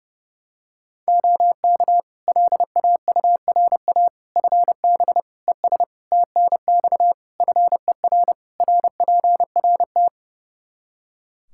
Morse code, heard as a keyed tone, about 30 wpm, 700 Hz. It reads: OK LAURA FB ES TNX FER RPRT
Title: K1USN SST QSO Practice 30wpm